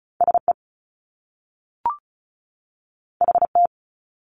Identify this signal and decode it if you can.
Morse code, heard as a keyed tone, about 36 wpm, 700 Hz. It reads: SE HT
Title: Top Two Letter Combinations in Words 35wpm